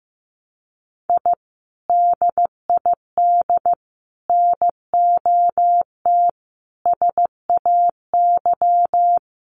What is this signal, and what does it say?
Morse code, heard as a keyed tone, about 15 wpm, 700 Hz. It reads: I DID NOT SAY
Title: Sets of 5 Words Encore2 15wpm